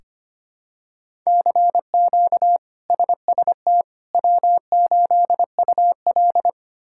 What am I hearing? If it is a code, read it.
Morse code, heard as a keyed tone, about 25 wpm, 700 Hz. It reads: CQ SST W8UL
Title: K1USN SST QSO Practice - 25wpm